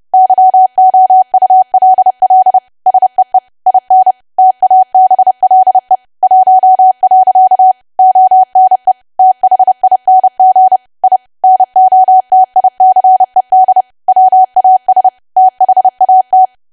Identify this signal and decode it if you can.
Morse code, heard as a keyed tone, about 30 wpm, 750 Hz. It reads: YOULL SEE IN TABLE 1. ONE THING I NOTICED WAS THAT
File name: cwa / W1AW-Level3 / 30WPM